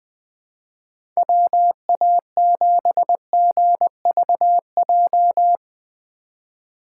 Morse code, heard as a keyed tone, about 20 wpm, 700 Hz. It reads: WA7GVJ
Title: USA Call Signs - 2x Character Spacing - 20wpm